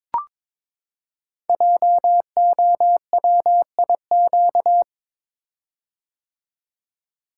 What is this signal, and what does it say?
Morse code, heard as a keyed tone, about 22 wpm, 700 Hz. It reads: JOWIQ
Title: Code Groups - 5 Letters - 22wpm